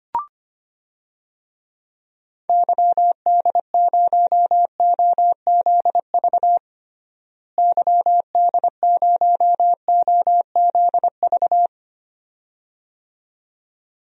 Morse code, heard as a keyed tone, about 25 wpm, 700 Hz. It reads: YD0OZV YD0OZV
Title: DXCC-100 Call Signs - 6 Characters - Repeated 2x - 25wpm